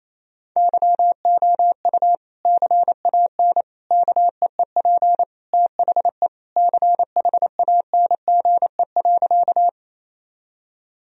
Morse code, heard as a keyed tone, about 28 wpm, 700 Hz. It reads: YOU CAN KEEP THE CHANGE.
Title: Sentences from Top 1000 Words - Review Part 1 - 28wpm